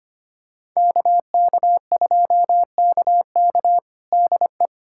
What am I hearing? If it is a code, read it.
Morse code, heard as a keyed tone, about 25 wpm, 700 Hz. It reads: KK2KK DE
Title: USA Callsign DE Callsign - 5 Characters - 25wpm